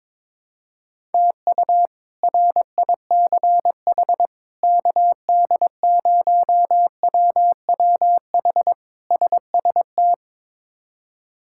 Morse code, heard as a keyed tone, about 22 wpm, 700 Hz. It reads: TU RICH KD0WWH SST
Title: K1USN SST QSO Practice - 22wpm